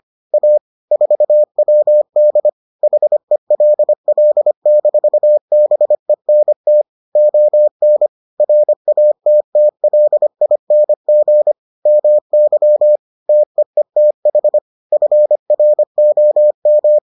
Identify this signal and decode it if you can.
Morse code, heard as a keyed tone, about 25 wpm, 600 Hz. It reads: A 4WD HELL-BENT ON RATTLING MY TEETH FROM